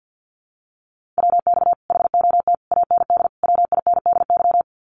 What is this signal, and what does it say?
Morse code, heard as a keyed tone, about 48 wpm, 700 Hz. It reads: WX HOT AND WINDY
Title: Common QSO Phrases Encore 50wpm